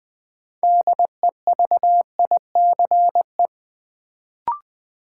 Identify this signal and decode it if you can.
Morse code, heard as a keyed tone, about 20 wpm, 700 Hz. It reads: DEVICE
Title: Sentences from Top 1500 Words - New Word Drill - 20wpm